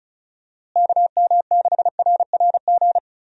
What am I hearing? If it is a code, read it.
Morse code, heard as a keyed tone, about 34 wpm, 700 Hz. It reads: KM6RRG
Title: USA Call Signs - 6 Characters - Repeated 2x - 35wpm